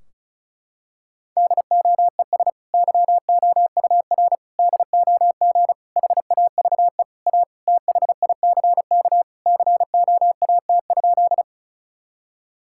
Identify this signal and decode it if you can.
Morse code, heard as a keyed tone, about 36 wpm, 700 Hz. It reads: DOES YOUR DOG HAVE A THICK COAT?
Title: Sentences from Top 1600 Words 35wpm